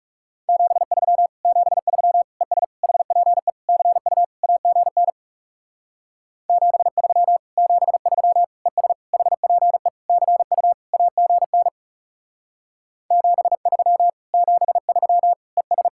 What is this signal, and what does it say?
Morse code, heard as a keyed tone, about 36 wpm, 700 Hz. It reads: 73 73 ES HPE CU AGN 73 73 ES HPE CU AGN 73 73 ES
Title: Speed-Racing Common QSO Phrases Encore - 30wpm